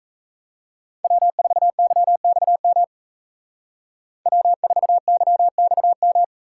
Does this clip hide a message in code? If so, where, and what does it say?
Yes: Morse code from 1.0 to 6.2 s, heard as a keyed tone, about 36 wpm, 700 Hz. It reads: W4YXM W4YXM